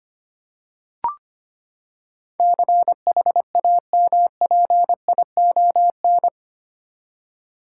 Morse code, heard as a keyed tone, about 25 wpm, 700 Hz. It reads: CHAMPION
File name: Eight Letter Words - 25wpm